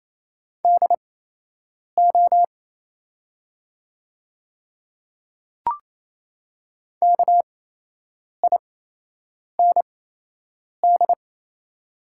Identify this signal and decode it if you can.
Morse code, heard as a keyed tone, about 27 wpm, spaced as for 8 wpm, 690 Hz. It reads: DO KIND
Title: 8x Character Spacing - Top 300 Words - 28wpm